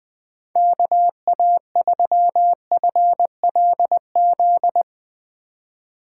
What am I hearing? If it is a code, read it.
Morse code, heard as a keyed tone, about 20 wpm, 700 Hz. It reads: KA3FLZ
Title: USA Call Signs - 6 Characters - 2x Character Spacing - 20wpm